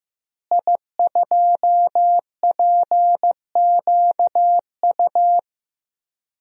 Morse code, heard as a keyed tone, about 15 wpm, 700 Hz. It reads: I2PQU
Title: DXCC-100 Call Signs - 5 Characters - Repeated 3x - 15wpm